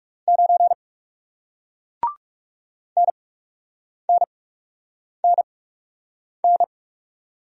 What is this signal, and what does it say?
Morse code, heard as a keyed tone, about 40 wpm, spaced as for 8 wpm, 700 Hz. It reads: 9 NNNN